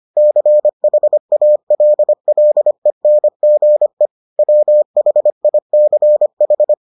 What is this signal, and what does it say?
Morse code, heard as a keyed tone, about 25 wpm, 600 Hz. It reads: CHALLENGE WHICH